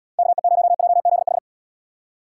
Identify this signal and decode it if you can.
Morse code, heard as a keyed tone, about 71 wpm, 700 Hz. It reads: KE0JZH